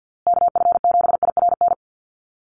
Morse code, heard as a keyed tone, about 50 wpm, 700 Hz. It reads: KF7IRN
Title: USA Call Signs Repeated 2x 50wpm